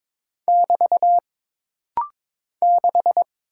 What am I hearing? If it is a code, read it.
Morse code, heard as a keyed tone, about 22 wpm, 700 Hz. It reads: <BT> 6
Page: This rapid-fire timing is geared to push your ICR (Instant Character Recognition) skills.